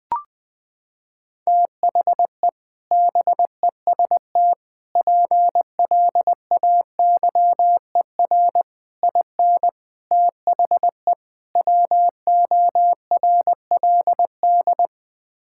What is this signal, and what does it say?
Morse code, heard as a keyed tone, about 20 wpm, 700 Hz. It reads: THE BEST PLAYER IN THE WORLD
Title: Sets of 6 Words - 20wpm